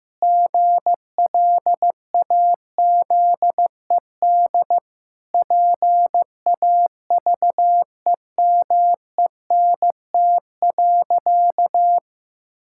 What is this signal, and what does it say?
Morse code, heard as a keyed tone, about 14 wpm, 700 Hz. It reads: GLAZED PAVEMENT.
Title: Pangram Challenge v2 - 15wpm